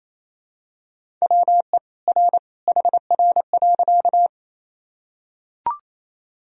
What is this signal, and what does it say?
Morse code, heard as a keyed tone, about 28 wpm, 700 Hz. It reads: WE R HR.
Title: Sentences from Top 200 Words and Abbreviations 28wpm